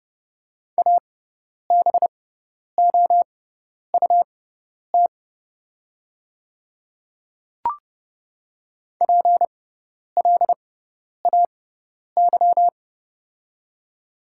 Morse code, heard as a keyed tone, about 30 wpm, spaced as for 10 wpm, 690 Hz. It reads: ABOUT PLAY
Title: 6x Character Spacing - Top 500 Words - 30wpm